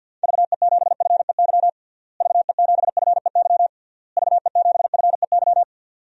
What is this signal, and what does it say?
Morse code, heard as a keyed tone, about 48 wpm, 700 Hz. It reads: VE7FEY VE7FEY VE7FEY
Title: DXCC-100 Call Signs - 6 Characters - Repeated 3x - 50wpm